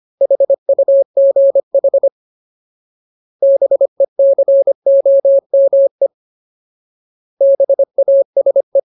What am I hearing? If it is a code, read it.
Morse code, heard as a keyed tone, about 25 wpm, 550 Hz. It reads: HUGH BECOME BASE
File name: Ultimate Word List / 25WPM Ultimate